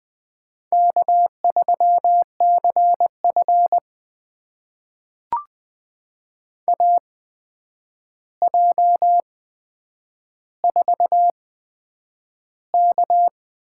Morse code, heard as a keyed tone, about 20 wpm, spaced as for 9 wpm, 700 Hz. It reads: K3CF AJ4K
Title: USA Call Signs - 4 Characters - 8x Character Spacing - 20wpm